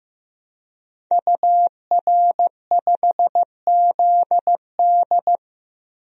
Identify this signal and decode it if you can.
Morse code, heard as a keyed tone, about 15 wpm, 700 Hz. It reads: UR5ZD
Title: DXCC-100 Call Signs - 5 Characters - Repeated 2x - 15wpm